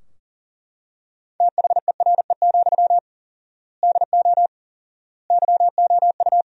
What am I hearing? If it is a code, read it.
Morse code, heard as a keyed tone, about 40 wpm, 700 Hz. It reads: THERE, DO YOU
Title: 4x Word-Spacing - Sentences from Top 300 Words - 40wpm